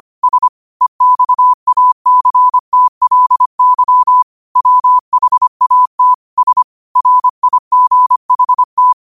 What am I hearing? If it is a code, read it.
Morse code, heard as a keyed tone, about 25 wpm, 1000 Hz. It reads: S EXACTLY WHAT S RIGHT
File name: qotdfiles / 25wpm